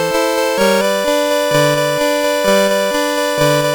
There is no code.